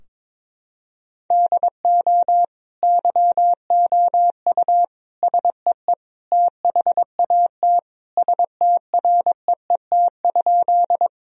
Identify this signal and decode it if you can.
Morse code, heard as a keyed tone, about 22 wpm, 700 Hz. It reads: DO YOU SEE THAT STREET?
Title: Sentences from Top 1000 Words - Review Part 3 - 22wpm